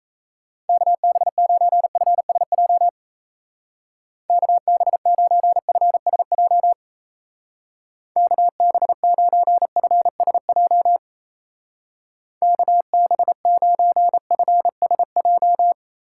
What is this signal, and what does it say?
Morse code, heard as a keyed tone, about 33 wpm, 700 Hz. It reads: KB9FSJ KB9FSJ KB9FSJ KB9FSJ
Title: Speed Racing - USA 2x3 Call Signs 28wpm